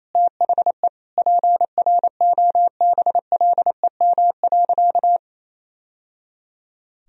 Morse code, heard as a keyed tone, about 28 wpm, 700 Hz. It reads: THE PROBLEM.
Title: Sentences from Top 1000 Words - Review Part 2 - 28wpm